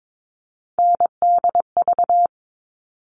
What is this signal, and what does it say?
Morse code, heard as a keyed tone, about 22 wpm, 700 Hz. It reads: NDV
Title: All Cut Numbers - 3-Digit 22wpm